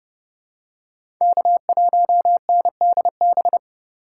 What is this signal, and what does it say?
Morse code, heard as a keyed tone, about 30 wpm, 700 Hz. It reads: K1NDB
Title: USA Call Signs - 5 Characters - 8x Character Spacing - 30wpm